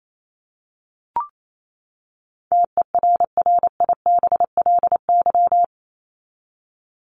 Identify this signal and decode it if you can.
Morse code, heard as a keyed tone, about 28 wpm, 700 Hz. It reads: TERRIBLY
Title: Top 1751-2000 Words 28wpm